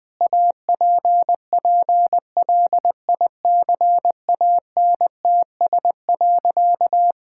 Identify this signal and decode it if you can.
Morse code, heard as a keyed tone, about 21 wpm, 700 Hz. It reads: APPLICANTS.
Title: Sentences with 6-Word Phrases - 20wpm